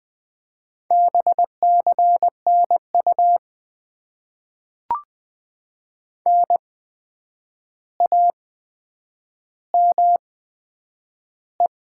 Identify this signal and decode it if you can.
Morse code, heard as a keyed tone, about 20 wpm, spaced as for 9 wpm, 700 Hz. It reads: BCNU NAME
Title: Top 90 QSO Elements - 8x Character Spacing - 20wpm